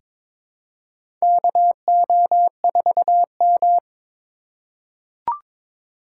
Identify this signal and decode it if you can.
Morse code, heard as a keyed tone, about 22 wpm, 700 Hz. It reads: KO4M
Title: USA Call Signs - 4 Characters - 4x Character Spacing - 22wpm